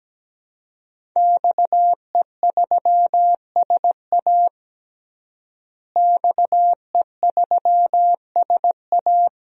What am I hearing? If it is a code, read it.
Morse code, heard as a keyed tone, about 17 wpm, 700 Hz. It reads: XE3SA XE3SA
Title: DXCC-100 Call Signs - 5 Characters - Repeated 2x - 17wpm